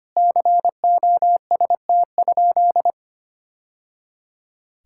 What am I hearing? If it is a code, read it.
Morse code, heard as a keyed tone, about 25 wpm, 700 Hz. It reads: COST?
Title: Sentences from Top 1000 Words - Review Part 5 - 25wpm